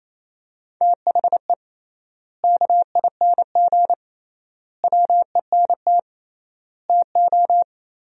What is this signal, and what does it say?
Morse code, heard as a keyed tone, about 28 wpm, 700 Hz. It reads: THE KING WENT TO
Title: 3x Word-Spacing - Sentences from Top 400 Words - 28wpm